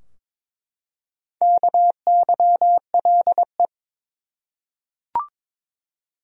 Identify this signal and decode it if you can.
Morse code, heard as a keyed tone, about 22 wpm, 700 Hz. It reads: KYLE
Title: USA Common Names - Max 4 Letters - 22wpm